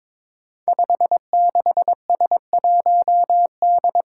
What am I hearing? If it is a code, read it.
Morse code, heard as a keyed tone, about 22 wpm, 700 Hz. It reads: 56S1D
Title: Five Challenging Letter-Number Combinations 22wpm